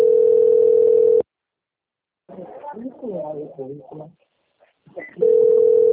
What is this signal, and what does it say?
Ringback tone (North America), 440 + 480 Hz, with speech.
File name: out- - - - - .6083